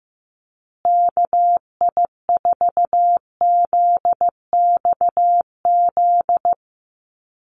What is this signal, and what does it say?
Morse code, heard as a keyed tone, about 15 wpm, 700 Hz. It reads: KI4ZXZ
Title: USA Call Signs Repeated 2x 15wpm